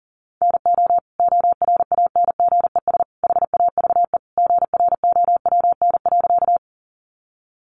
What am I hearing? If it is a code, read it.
Morse code, heard as a keyed tone, about 40 wpm, 700 Hz. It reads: NO ORANGES HAVE GROWN.
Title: Easy Phrases 40wpm